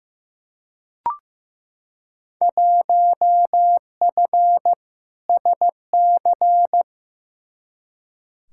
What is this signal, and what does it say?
Morse code, heard as a keyed tone, about 15 wpm, 700 Hz. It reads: 1F SC
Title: Field Day Exchanges Encore - 15wpm